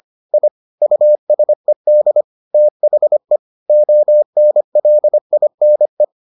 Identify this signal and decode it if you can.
Morse code, heard as a keyed tone, about 25 wpm, 600 Hz. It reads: I USED THE ONLINE